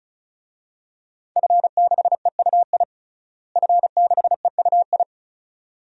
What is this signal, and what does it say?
Morse code, heard as a keyed tone, about 36 wpm, 700 Hz. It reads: F6EUI F6EUI
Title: DXCC-100 Call Signs - 5 Characters - Repeated 2x - 35wpm